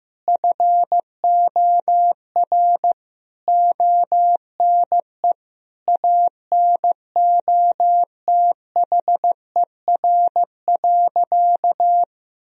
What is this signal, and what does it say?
Morse code, heard as a keyed tone, about 15 wpm, 700 Hz. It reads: FOR ONE ANOTHER.
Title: Sentences from Top 1000 Words - Review Part 1 - 15wpm